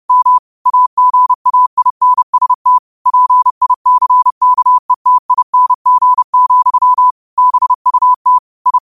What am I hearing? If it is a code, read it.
Morse code, heard as a keyed tone, about 30 wpm, 1000 Hz. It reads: M AGAINST PICKETING, BUT I
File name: qotdfiles / 30wpm